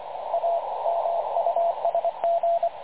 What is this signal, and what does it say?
Morse code, heard as a keyed tone, about 25 wpm, 680 Hz, with radio noise. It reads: A MUSG
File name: dlazaawansowanych / nagraniazpasma / websdr recording 3567.0kHz